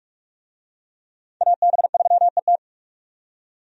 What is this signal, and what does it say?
Morse code, heard as a keyed tone, about 46 wpm, 700 Hz. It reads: AB3ET
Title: USA Call Signs - 5 Characters - 45wpm